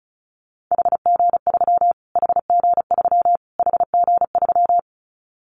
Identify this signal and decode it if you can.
Morse code, heard as a keyed tone, about 34 wpm, 700 Hz. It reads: HG3 HG3 HG3
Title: Callsign Prefixes for Top 25 DXCC - Repeated 3x - 35wpm